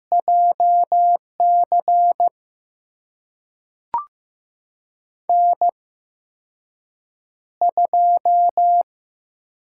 Morse code, heard as a keyed tone, about 15 wpm, spaced as for 6 wpm, 700 Hz. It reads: JC N2
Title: USA Call Signs - 5 Characters - 8x Character Spacing - 15wpm